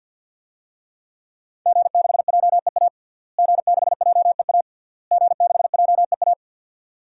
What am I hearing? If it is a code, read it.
Morse code, heard as a keyed tone, about 52 wpm, 700 Hz. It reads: G6JEA G6JEA G6JEA
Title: DXCC-100 Call Signs - 5 Characters - Repeated 3x - 50wpm